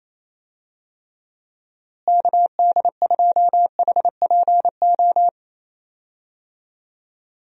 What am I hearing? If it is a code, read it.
Morse code, heard as a keyed tone, about 28 wpm, 700 Hz. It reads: KD2HPO
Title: USA Call Signs - 6 Characters - Repeated 2x - 28wpm